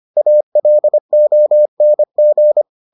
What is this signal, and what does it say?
Morse code, heard as a keyed tone, about 25 wpm, 600 Hz. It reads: ALONG